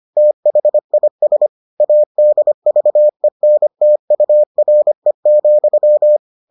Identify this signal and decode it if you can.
Morse code, heard as a keyed tone, about 25 wpm, 600 Hz. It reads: THIS ADVENTURE,